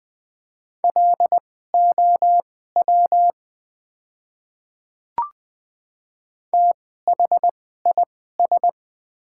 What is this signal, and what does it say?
Morse code, heard as a keyed tone, about 20 wpm, spaced as for 14 wpm, 690 Hz. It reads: LOW THIS
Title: 2x Character Spacing - Top 300 Words - 20wpm